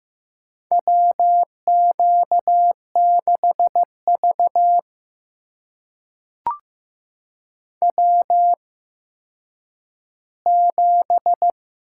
Morse code, heard as a keyed tone, about 15 wpm, spaced as for 9 wpm, 700 Hz. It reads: WQ6V W7
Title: USA Call Signs - 4 Characters - 8x Character Spacing - 15wpm